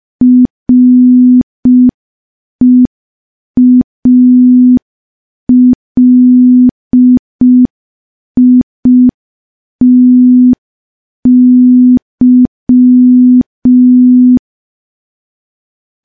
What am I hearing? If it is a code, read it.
Morse code, heard as a keyed tone, about 5 wpm, 250 Hz. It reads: REALITY
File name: morse / morse-training-05WPM